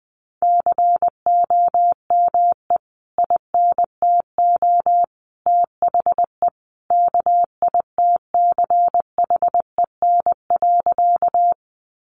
Morse code, heard as a keyed tone, about 20 wpm, 700 Hz. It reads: COME INTO THE KITCHEN.
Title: Sentences from Top 1400 Words 20wpm